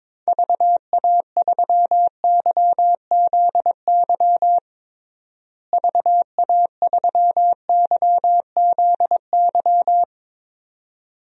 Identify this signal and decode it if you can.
Morse code, heard as a keyed tone, about 22 wpm, 700 Hz. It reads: VA3YZY VA3YZY
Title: DXCC-100 Call Signs - 6 Characters - Repeated 2x - 22wpm